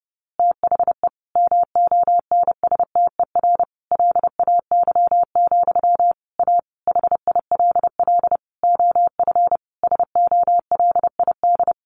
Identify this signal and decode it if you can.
Morse code, heard as a keyed tone, about 30 wpm, 700 Hz. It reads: THE MONSTER LAY, A HILL OF SOLID
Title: A Sound Of Thunder 30wpm